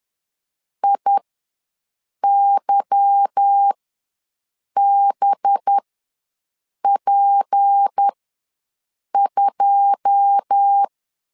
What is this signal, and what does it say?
Morse code, heard as a keyed tone, about 10 wpm, spaced as for 6 wpm, 800 Hz. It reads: IYBP2
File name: LEZIONI / 9 2 ws 10wpm 4130